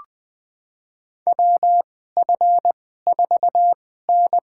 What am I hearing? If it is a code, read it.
Morse code, heard as a keyed tone, about 20 wpm, spaced as for 15 wpm, 700 Hz. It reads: WF4N
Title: USA Call Signs - 4 Characters - 2x Character Spacing - 20wpm